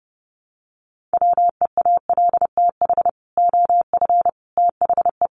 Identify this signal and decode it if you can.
Morse code, heard as a keyed tone, about 28 wpm, 700 Hz. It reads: WEALTH OF THE
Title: Sentences from Top 1200 Words 30wpm